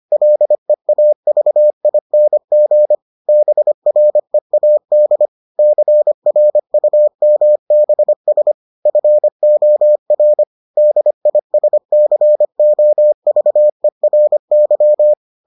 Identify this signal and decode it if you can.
Morse code, heard as a keyed tone, about 25 wpm, 600 Hz. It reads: LEAVING BREAD CRUMBS FOR DISCOVERY